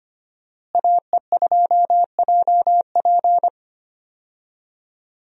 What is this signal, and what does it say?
Morse code, heard as a keyed tone, about 25 wpm, 700 Hz. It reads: AE2JP
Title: USA Call Signs - 5 Characters - Repeated 3x - 25wpm